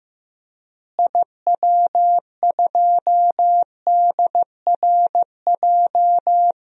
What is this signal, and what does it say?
Morse code, heard as a keyed tone, about 15 wpm, 700 Hz. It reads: IW2DRJ
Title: DXCC-100 Call Signs - 6 Characters - Repeated 3x - 15wpm